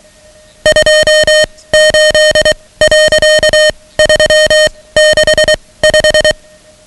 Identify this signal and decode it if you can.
Morse code, heard as a keyed tone, about 23 wpm, spaced as for 17 wpm, 620 Hz, with noise. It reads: 28.365
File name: ham radio / CodeQuick / Code Quick 10